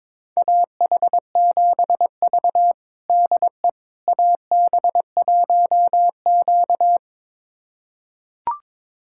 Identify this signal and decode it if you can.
Morse code, heard as a keyed tone, about 21 wpm, 700 Hz. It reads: AH7V DE AB1Q
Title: USA Callsign DE Callsign - 4 Characters - 22wpm